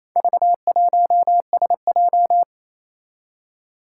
Morse code, heard as a keyed tone, about 28 wpm, 700 Hz. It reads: V1SJ
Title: Four Challenging Letter-Number Combinations 28wpm